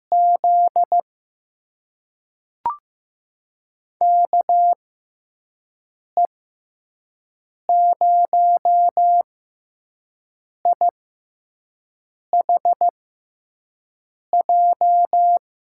Morse code, heard as a keyed tone, about 15 wpm, spaced as for 5 wpm, 700 Hz. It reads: Z KE0IHJ
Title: USA Call Signs - 6 Characters - 6x Character Spacing - 15wpm